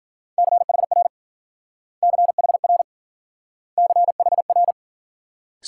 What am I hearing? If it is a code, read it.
Morse code, heard as a keyed tone, about 49 wpm, 700 Hz. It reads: CHR CHR CHR
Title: Speed-Racing Multi-letter Phonemes - 40wpm